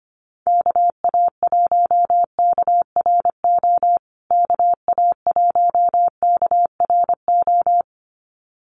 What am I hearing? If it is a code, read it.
Morse code, heard as a keyed tone, about 25 wpm, 700 Hz. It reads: KA1KRO KA1KRO
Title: USA Call Signs Repeated 2x 25wpm